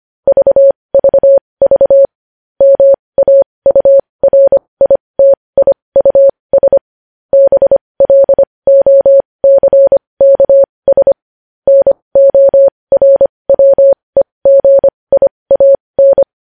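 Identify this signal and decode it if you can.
Morse code, heard as a keyed tone, about 25 wpm, spaced as for 20 wpm, 550 Hz. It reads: VVV MAURITIUS BLOCKS NORWEGIAN
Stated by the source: Morse code news headlines sent at 20.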